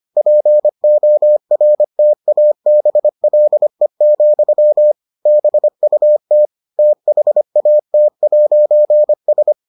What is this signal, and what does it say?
Morse code, heard as a keyed tone, about 25 wpm, 600 Hz. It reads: PORTABLE, BUT THAT'S